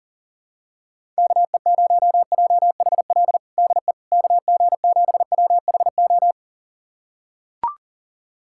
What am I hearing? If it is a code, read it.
Morse code, heard as a keyed tone, about 37 wpm, 700 Hz. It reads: KE0JHL DE KG7WHO
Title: USA Callsign DE Callsign 40wpm